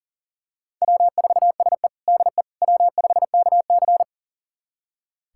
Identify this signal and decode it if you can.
Morse code, heard as a keyed tone, about 40 wpm, 700 Hz. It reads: W4SE DE W5KC
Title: USA Callsign DE Callsign - 4 Characters - 40wpm